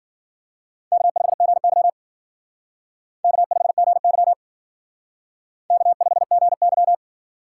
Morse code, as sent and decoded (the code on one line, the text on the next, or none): -.- ..... --. -.-- / -.- ..... --. -.-- / -.- ..... --. -.--
K5GY K5GY K5GY